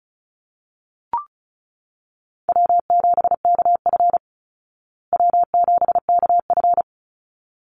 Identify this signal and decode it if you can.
Morse code, heard as a keyed tone, about 34 wpm, 700 Hz. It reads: W7KF W7KF
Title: Easy Call Signs Repeated 2x - 35wpm